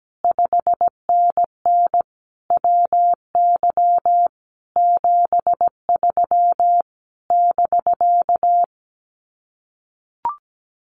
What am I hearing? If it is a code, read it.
Morse code, heard as a keyed tone, about 17 wpm, 700 Hz. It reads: 5NN WY 73 <BK>
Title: Common QSO Phrases Encore 17wpm